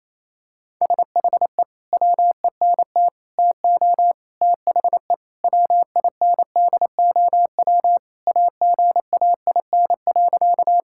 Morse code, heard as a keyed tone, about 28 wpm, 700 Hz. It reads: SHE WENT TO THE WINDOW AGAIN.
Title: Sentences from Top 1600 Words 28wpm